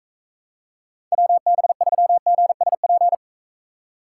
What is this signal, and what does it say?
Morse code, heard as a keyed tone, about 44 wpm, 700 Hz. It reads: WB3GSP